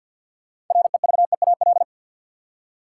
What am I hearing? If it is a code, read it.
Morse code, heard as a keyed tone, about 46 wpm, 700 Hz. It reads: REVEAL
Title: Six Letter Words - 50wpm